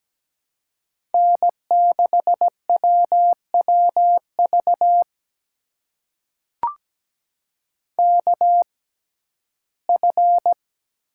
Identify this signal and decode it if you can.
Morse code, heard as a keyed tone, about 17 wpm, spaced as for 12 wpm, 700 Hz. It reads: N6WWV KF
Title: USA Call Signs - 5 Characters - 6x Character Spacing - 17wpm